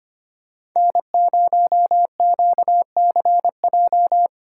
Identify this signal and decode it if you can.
Morse code, heard as a keyed tone, about 25 wpm, 700 Hz. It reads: N0QCJ
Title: USA Call Signs - 5 Characters - 2x Character Spacing - 25wpm